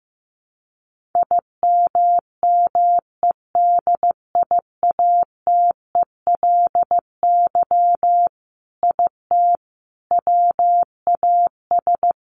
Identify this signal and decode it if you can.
Morse code, heard as a keyed tone, about 15 wpm, 700 Hz. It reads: IMMEDIATELY IT WAS
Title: Sentences from Top 1100 Words 15wpm